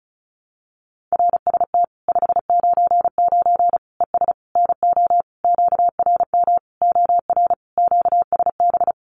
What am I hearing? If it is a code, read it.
Morse code, heard as a keyed tone, about 34 wpm, 700 Hz. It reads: RST 599 ES NO QRM OR QSB
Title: Common QSO Phrases 35wpm